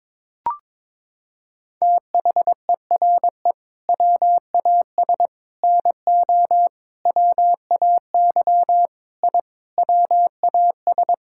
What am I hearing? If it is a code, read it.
Morse code, heard as a keyed tone, about 22 wpm, 700 Hz. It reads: THERE WAS NO WAY I WAS
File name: practice / 22wpm / Sets of 8 Words - 22wpm